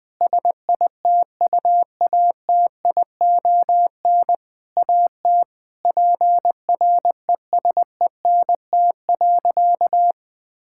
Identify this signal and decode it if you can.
Morse code, heard as a keyed tone, about 19 wpm, 700 Hz. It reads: SITUATION AT PRESENT.
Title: Sentences from Top 1600 Words 20wpm